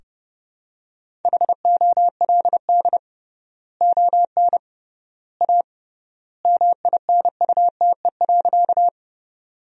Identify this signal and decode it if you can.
Morse code, heard as a keyed tone, about 30 wpm, 700 Hz. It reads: HOLD ON A MINUTE.
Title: 3x Word-Spacing - Sentences from Top 500 Words - 30wpm